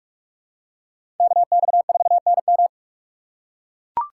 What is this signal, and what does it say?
Morse code, heard as a keyed tone, about 43 wpm, 700 Hz. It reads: KX4NM
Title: K1USN SST QSO Practice - 45wpm